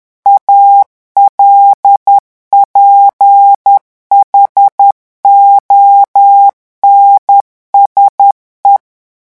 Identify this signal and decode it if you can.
Morse code, heard as a keyed tone, about 10 wpm, 800 Hz. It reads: ALPHONSE
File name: revision9 citations humoristiques vitesse 10 mots minute